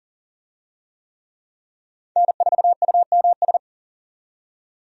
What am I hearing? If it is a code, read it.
Morse code, heard as a keyed tone, about 40 wpm, 700 Hz. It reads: N4UMS
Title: USA Call Signs - 5 Characters - 40wpm